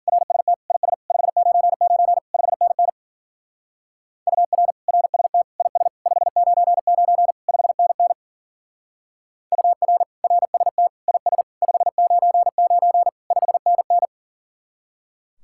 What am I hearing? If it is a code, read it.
Morse code, heard as a keyed tone, about 48 wpm, 700 Hz. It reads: RST IS 599 5NN UR RST IS 599 5NN UR RST IS 599 5NN
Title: Speed-Racing Common QSO Phrases Encore - 40wpm